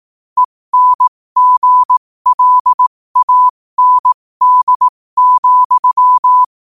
Morse code, heard as a keyed tone, about 18 wpm, spaced as for 15 wpm, 1000 Hz. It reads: ENGLAND,
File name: qotdfiles / 15wpm